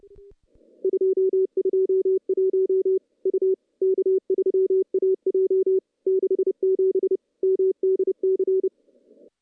Morse code, heard as a keyed tone, about 30 wpm, 380 Hz. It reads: 221 U K3AJ 67 MDC
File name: contests / CW-SS-k5tr- -0000